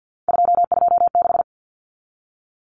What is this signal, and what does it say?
Morse code, heard as a keyed tone, about 50 wpm, 700 Hz. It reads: 226
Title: Numbers - 3-Digit 50wpm